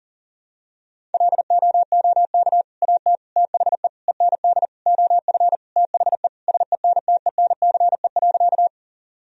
Morse code, heard as a keyed tone, about 37 wpm, 700 Hz. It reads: LOOK AT THE END OF THE SENTENCE.
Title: Sentences from Top 1000 Words - Review Part 1 - 40wpm